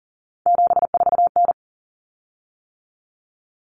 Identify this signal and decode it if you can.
Morse code, heard as a keyed tone, about 40 wpm, 700 Hz. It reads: 74N
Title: Common Cut Numbers - 3-Digit 40wpm